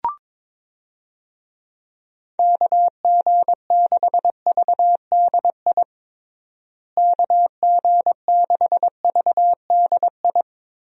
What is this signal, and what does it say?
Morse code, heard as a keyed tone, about 22 wpm, 700 Hz. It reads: KG6VDI KG6VDI
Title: USA Call Signs - 6 Characters - Repeated 3x - 22wpm